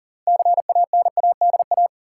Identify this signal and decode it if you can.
Morse code, heard as a keyed tone, about 40 wpm, 700 Hz. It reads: CANADA
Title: Sets of 5 Words - New Word Drill - 40wpm